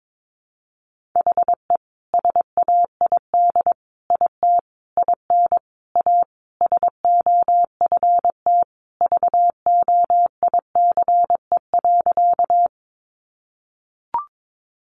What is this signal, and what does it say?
Morse code, heard as a keyed tone, about 22 wpm, 700 Hz. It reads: HE SAID IT IN A SOFT VOICE.
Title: Sentences from Top 1500 Words 22wpm